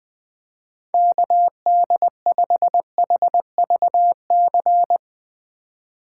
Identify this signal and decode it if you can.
Morse code, heard as a keyed tone, about 22 wpm, 700 Hz. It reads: KD5HVC
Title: Speed Racing - USA 2x3 Call Signs 20wpm